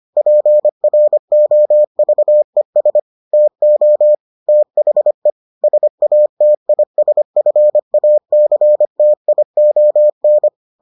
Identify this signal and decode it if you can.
Morse code, heard as a keyed tone, about 25 wpm, 600 Hz. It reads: PROVES TO THE SATISFACTION